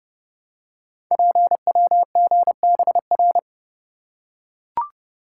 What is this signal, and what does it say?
Morse code, heard as a keyed tone, about 30 wpm, 700 Hz. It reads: PWGBR
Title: Code Groups - 5 Letters - 30wpm